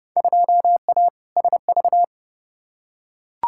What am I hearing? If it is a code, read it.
Morse code, heard as a keyed tone, about 30 wpm, 700 Hz. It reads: 2A SV
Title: Field Day Exchanges - 30wpm